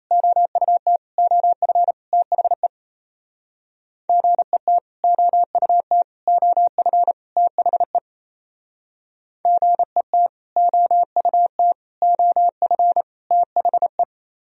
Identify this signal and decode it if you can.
Morse code, heard as a keyed tone, about 34 wpm, 700 Hz. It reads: OUT OF THE GET OUT OF THE GET OUT OF THE
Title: Speed-Racing - Sets of 4 Words 28wpm